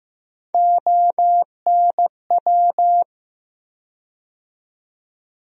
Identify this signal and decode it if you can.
Morse code, heard as a keyed tone, about 15 wpm, 700 Hz. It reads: ONW
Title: USA Call Sign Postfixes - 3 Letters 15wpm